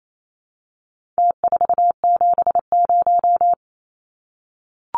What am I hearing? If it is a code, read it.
Morse code, heard as a keyed tone, about 28 wpm, 700 Hz. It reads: T470
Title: Numbers - Contest Style - 4-Digit 28wpm